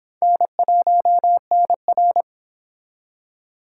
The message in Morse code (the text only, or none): N1NR